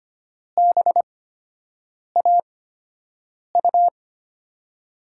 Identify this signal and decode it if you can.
Morse code, heard as a keyed tone, about 25 wpm, spaced as for 7 wpm, 700 Hz. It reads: BAU
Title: USA Call Signs - 8x Character Spacing - 25wpm